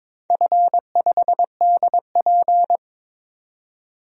Morse code, heard as a keyed tone, about 22 wpm, 700 Hz. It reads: F5DP
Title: DXCC-100 Call Signs - 4 Characters - 22wpm